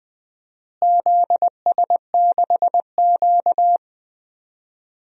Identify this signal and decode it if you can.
Morse code, heard as a keyed tone, about 20 wpm, 700 Hz. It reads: ZS6Q
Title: DXCC-100 Call Signs - 4 Characters - Repeated 2x - 20wpm